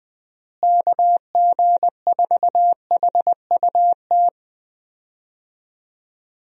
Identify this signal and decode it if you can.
Morse code, heard as a keyed tone, about 21 wpm, 700 Hz. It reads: KG4HUT
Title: USA Call Signs - 6 Characters - Repeated 3x - 20wpm